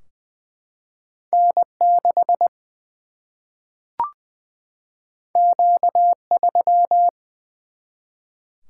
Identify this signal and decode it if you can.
Morse code, heard as a keyed tone, about 20 wpm, 700 Hz. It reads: N 6 Q 3
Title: Single-Letter Single-Number 20wpm